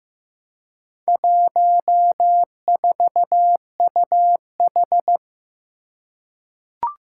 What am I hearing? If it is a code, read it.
Morse code, heard as a keyed tone, about 15 wpm, 700 Hz. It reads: 14UH
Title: Four Challenging Letter-Number Combinations 15wpm